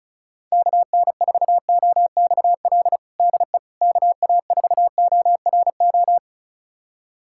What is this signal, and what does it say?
Morse code, heard as a keyed tone, about 36 wpm, 700 Hz. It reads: KN4OXL DE KA4ORO
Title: USA Callsign DE Callsign 35wpm